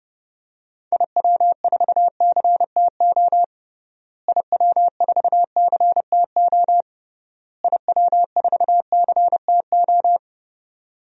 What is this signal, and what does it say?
Morse code, heard as a keyed tone, about 30 wpm, 700 Hz. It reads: IW4CTO IW4CTO IW4CTO
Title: DXCC-100 Call Signs - 6 Characters - Repeated 3x - 30wpm